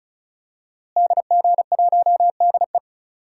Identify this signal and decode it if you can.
Morse code, heard as a keyed tone, about 36 wpm, 700 Hz. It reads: DG1DE
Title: DXCC-100 Call Signs - 5 Characters - Repeated 3x - 35wpm